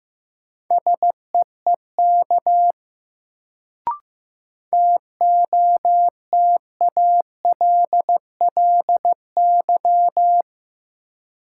Morse code, heard as a keyed tone, about 15 wpm, 700 Hz. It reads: SEEK TOTALLY
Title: Sentences from Top 1600 Words - New Word Drill - 15wpm